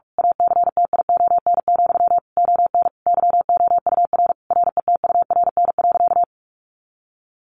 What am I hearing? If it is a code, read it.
Morse code, heard as a keyed tone, about 44 wpm, 700 Hz. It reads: ACTION, ON YOUR RETURN.
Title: A Sound Of Thunder 45wpm